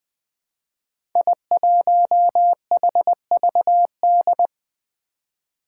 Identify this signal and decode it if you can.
Morse code, heard as a keyed tone, about 20 wpm, 700 Hz. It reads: I1HVD
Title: Five Challenging Letter-Number Combinations 20wpm